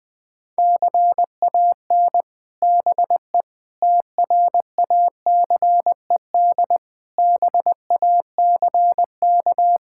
Morse code, heard as a keyed tone, about 20 wpm, 700 Hz. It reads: CAN BE TRACED BACK
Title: Sets of 6 Words - 20wpm